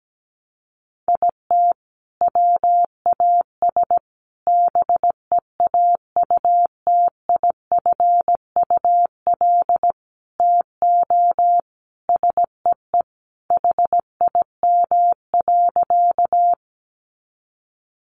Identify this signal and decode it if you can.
Morse code, heard as a keyed tone, about 17 wpm, 700 Hz. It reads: IT WAS BEAUTIFUL TO SEE HIM.
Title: Sentences from Top 900 Words 17wpm